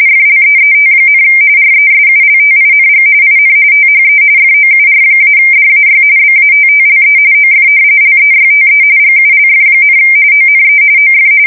RTTY, about 45.45 baud. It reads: FLEET, RESULTING IN THE FOUR-DAY BATTLE OF LEYTE GULF, ONE OF
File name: fsk170 / usnavy4